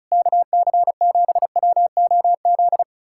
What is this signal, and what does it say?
Morse code, heard as a keyed tone, about 40 wpm, 700 Hz. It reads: KC7WOZ